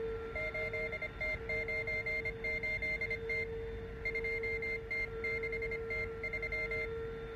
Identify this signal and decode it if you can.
Morse code, heard as a keyed tone, about 25 wpm, 2100 Hz, with noise. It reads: 8T98T 2T6T3